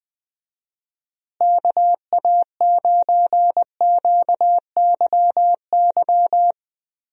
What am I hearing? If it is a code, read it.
Morse code, heard as a keyed tone, about 20 wpm, 700 Hz. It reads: KA9QYY
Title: USA Call Signs - 6 Characters - Repeated 2x - 20wpm